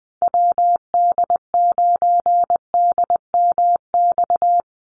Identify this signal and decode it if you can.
Morse code, heard as a keyed tone, about 20 wpm, 700 Hz. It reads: WD9DMX
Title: USA Call Signs 20wpm